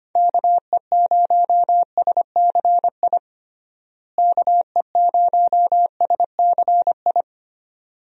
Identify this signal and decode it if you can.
Morse code, heard as a keyed tone, about 25 wpm, 700 Hz. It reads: KE0SCI KE0SCI
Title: USA Call Signs - 6 Characters - Repeated 2x - 25wpm